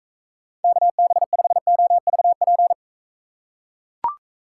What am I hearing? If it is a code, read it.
Morse code, heard as a keyed tone, about 38 wpm, 700 Hz. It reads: KB5OVP